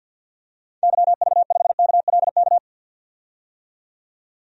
Morse code, heard as a keyed tone, about 47 wpm, 700 Hz. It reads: YV5KLK
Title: DXCC-100 Call Signs - 6 Characters - 50wpm